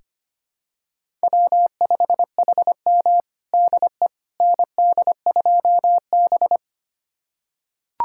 Morse code, heard as a keyed tone, about 25 wpm, 700 Hz. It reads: W5HM DE ND2B
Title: USA Callsign DE Callsign - 4 Characters - 25wpm